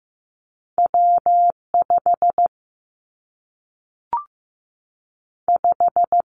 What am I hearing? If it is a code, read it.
Morse code, heard as a keyed tone, about 15 wpm, 700 Hz. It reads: W 5 5
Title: Two Challenging Letter-Number Combinations 15wpm